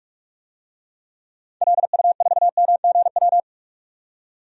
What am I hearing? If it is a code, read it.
Morse code, heard as a keyed tone, about 42 wpm, 700 Hz. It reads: LU4MGW
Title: DXCC-100 Call Signs - 6 Characters - 45wpm